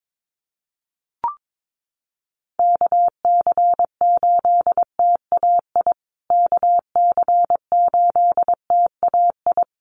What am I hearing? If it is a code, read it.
Morse code, heard as a keyed tone, about 22 wpm, 700 Hz. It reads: KC8TAI KC8TAI
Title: USA Call Signs Repeated 3x 22wpm